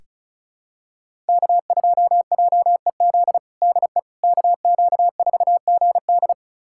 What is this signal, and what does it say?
Morse code, heard as a keyed tone, about 36 wpm, 700 Hz. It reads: K2JEZ DE KQ4GD
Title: USA Callsign DE Callsign - 5 Characters - 35wpm